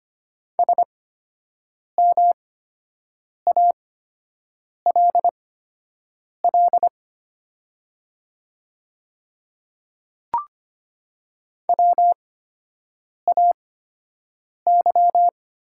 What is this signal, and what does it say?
Morse code, heard as a keyed tone, about 25 wpm, spaced as for 7 wpm, 700 Hz. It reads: SMALL WAY
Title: 8x Character Spacing - Top 200 Words - 25wpm